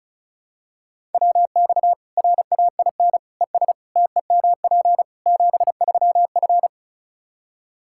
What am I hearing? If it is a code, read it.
Morse code, heard as a keyed tone, about 34 wpm, 700 Hz. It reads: WX RAIN ES TEMP 73F
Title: Practice QSO - Exchange 2 - Info - 35wpm